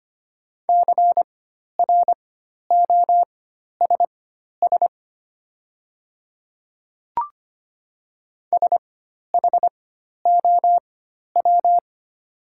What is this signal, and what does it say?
Morse code, heard as a keyed tone, about 25 wpm, spaced as for 12 wpm, 700 Hz. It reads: CROSS SHOW
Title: 4x Character Spacing - Top 500 Words - 25wpm